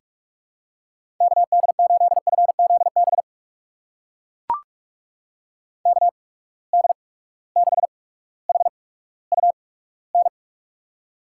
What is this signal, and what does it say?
Morse code, heard as a keyed tone, about 42 wpm, spaced as for 19 wpm, 700 Hz. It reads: KD8FZB KD6HUN
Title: USA Call Signs - 6 Characters - 8x Character Spacing - 45wpm